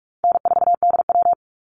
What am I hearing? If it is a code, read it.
Morse code, heard as a keyed tone, about 44 wpm, 700 Hz. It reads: N4DW
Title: Call Signs Repeated 2x - 45wpm